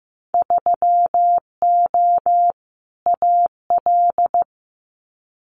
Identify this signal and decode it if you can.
Morse code, heard as a keyed tone, about 15 wpm, 700 Hz. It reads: 3O AL
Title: Winter Field Day Exchanges 15wpm